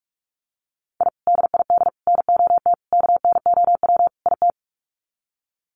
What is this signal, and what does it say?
Morse code, heard as a keyed tone, about 44 wpm, 690 Hz. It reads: I DID NOT KNOW IT
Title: Sets of 5 Words Encore2 45wpm